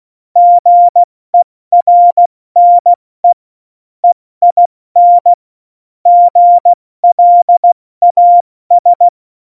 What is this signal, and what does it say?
Morse code, heard as a keyed tone, about 16 wpm, spaced as for 14 wpm, 700 Hz. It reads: GERNE EIN GLAS
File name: Morsekurs / Witze / Witz 23. April 2019, 70 F80